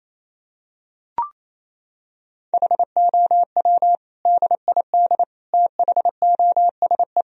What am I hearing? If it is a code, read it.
Morse code, heard as a keyed tone, about 28 wpm, 700 Hz. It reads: HOW DID THOSE
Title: Sentences from Top 1000 Words - Review Part 1 - 28wpm